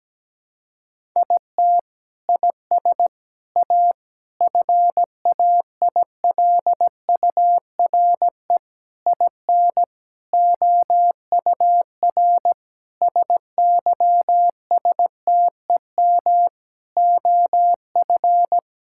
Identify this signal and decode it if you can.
Morse code, heard as a keyed tone, about 17 wpm, 700 Hz. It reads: IT IS A FAILURE IN OUR SYSTEM OF
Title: Sentences from Top 1000 Words - Review Part 6 - 17wpm